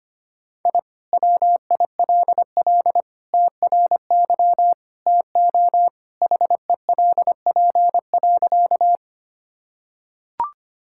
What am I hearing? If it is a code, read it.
Morse code, heard as a keyed tone, about 25 wpm, 700 Hz. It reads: I WILL TRY TO HELP.
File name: practice / 25wpm / Sentences with 3-Word Phrases - 25wpm